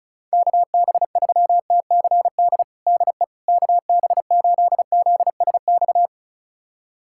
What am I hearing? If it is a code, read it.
Morse code, heard as a keyed tone, about 36 wpm, 690 Hz. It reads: KB3TCD DE KB8ZSX
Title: USA Callsign DE Callsign - 6 Characters - 35wpm